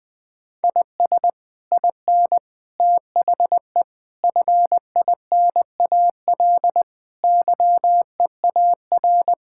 Morse code, heard as a keyed tone, about 20 wpm, 700 Hz. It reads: IS IN THE FINAL YEAR
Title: Sets of 8 Words - 20wpm